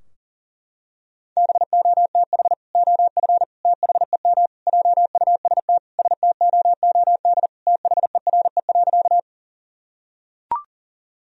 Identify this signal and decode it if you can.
Morse code, heard as a keyed tone, about 40 wpm, 700 Hz. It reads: BOTH OF THEM JUST STOOD THERE.
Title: Sentences from Top 1000 Words - Review Part 2 - 40wpm